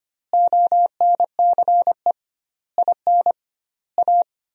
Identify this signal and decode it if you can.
Morse code, heard as a keyed tone, about 25 wpm, 700 Hz. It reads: ONCE IN A
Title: 2x Word-Spacing - Sentences from Top 500 Words - 25wpm